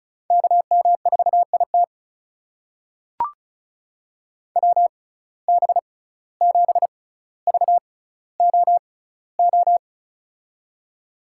Morse code, heard as a keyed tone, about 36 wpm, spaced as for 17 wpm, 700 Hz. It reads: KM4IT WB7VOO
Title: USA Call Signs - 6x Character Spacing - 35wpm